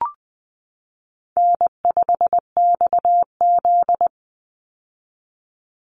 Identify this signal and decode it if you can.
Morse code, heard as a keyed tone, about 20 wpm, 700 Hz. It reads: N5XZ
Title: Easy Call Signs - 20wpm